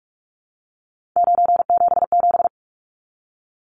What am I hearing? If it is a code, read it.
Morse code, heard as a keyed tone, about 44 wpm, 700 Hz. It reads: 977
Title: Numbers - 3-Digit 45wpm